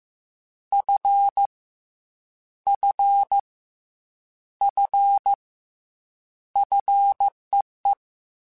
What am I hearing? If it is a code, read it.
Morse code, heard as a keyed tone, about 15 wpm, 800 Hz. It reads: F F F FEE
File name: Fs / Single F mp3s / Single F Prefixes 3x2 Twice @13